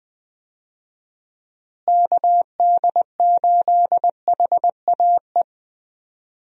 Morse code, heard as a keyed tone, about 20 wpm, 700 Hz. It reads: KD8HAE
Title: USA Call Signs - 6 Characters - Repeated 3x - 20wpm